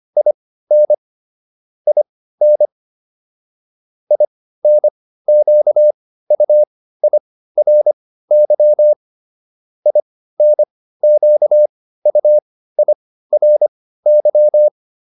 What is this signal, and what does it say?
Morse code, heard as a keyed tone, about 25 wpm, spaced as for 15 wpm, 600 Hz. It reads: IN IN INQUIRY INQUIRY